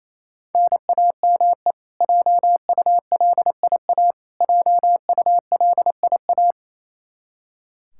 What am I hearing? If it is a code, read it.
Morse code, heard as a keyed tone, about 28 wpm, 700 Hz. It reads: NAME JULIA JULIA
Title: Practice QSO - Exchange 1 - Beginning - 28wpm